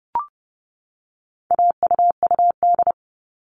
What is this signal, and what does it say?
Morse code, heard as a keyed tone, about 30 wpm, 700 Hz. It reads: AUUD
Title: All Cut Numbers - 4-Digit 30wpm